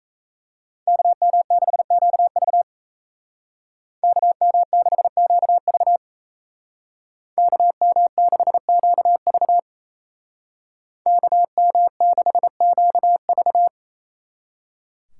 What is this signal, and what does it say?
Morse code, heard as a keyed tone, about 36 wpm, 700 Hz. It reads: KM6QV KM6QV KM6QV KM6QV